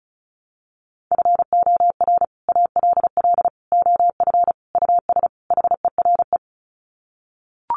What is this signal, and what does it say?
Morse code, heard as a keyed tone, about 34 wpm, 700 Hz. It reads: FOR ALL OF US HERE
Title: Sets of 5 Words 35wpm